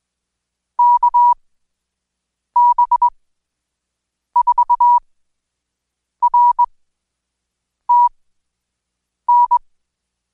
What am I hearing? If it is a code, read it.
Morse code, heard as a keyed tone, about 20 wpm, spaced as for 6 wpm, 970 Hz. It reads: KB4RTN
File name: Code Quick/AudioDownld2/Seg109-Day14-DayDrill 14